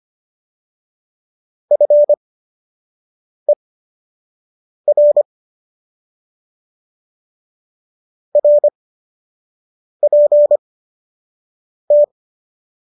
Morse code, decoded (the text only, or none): FER RPT